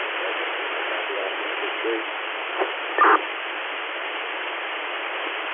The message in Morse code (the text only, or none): none